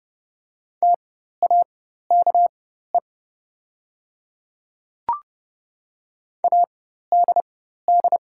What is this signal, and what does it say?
Morse code, heard as a keyed tone, about 30 wpm, spaced as for 14 wpm, 690 Hz. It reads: TAKE ADD
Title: 4x Character Spacing - Top 200 Words - 30wpm